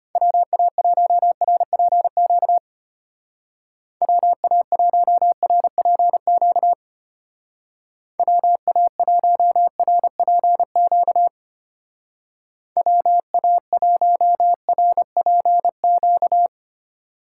Morse code, heard as a keyed tone, about 30 wpm, 700 Hz. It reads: WA1RPQ WA1RPQ WA1RPQ WA1RPQ
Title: Speed Racing - USA 2x3 Call Signs 25wpm